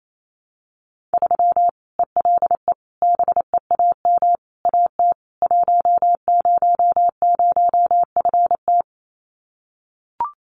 Morse code, heard as a keyed tone, about 28 wpm, 700 Hz. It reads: 3 ELE BEAM AT 100FT
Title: Common QSO Phrases 28wpm